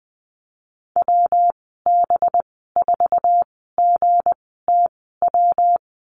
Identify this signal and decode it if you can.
Morse code, heard as a keyed tone, about 20 wpm, spaced as for 14 wpm, 700 Hz. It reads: WB4GTW
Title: 193 Lesson 31g - Letter v - Call Signs 20wpm